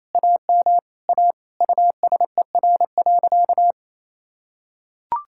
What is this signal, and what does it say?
Morse code, heard as a keyed tone, about 27 wpm, 700 Hz. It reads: AM A USER.
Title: Sentences from Top 1000 Words - Review Part 5 - 28wpm